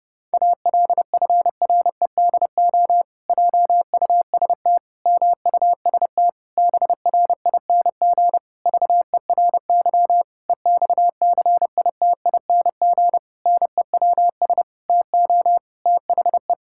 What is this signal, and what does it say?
Morse code, heard as a keyed tone, about 28 wpm, 700 Hz. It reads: ALFREDO JUST MUST BRING VERY EXCITING NEWS TO THE
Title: Pangram Challenge v2 - 30wpm